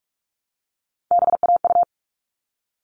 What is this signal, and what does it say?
Morse code, heard as a keyed tone, about 44 wpm, 700 Hz. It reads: BAU
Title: All Cut Numbers - 3-Digit 45wpm